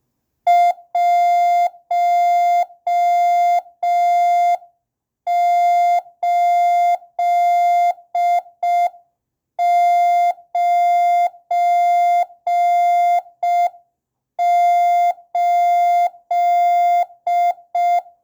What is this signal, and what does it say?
Morse code, heard as a keyed tone, about 5 wpm, 700 Hz. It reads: 1898